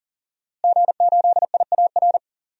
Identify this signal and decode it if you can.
Morse code, heard as a keyed tone, about 40 wpm, 700 Hz. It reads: G8IAR
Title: DXCC-100 Call Signs - 5 Characters - Repeated 2x - 40wpm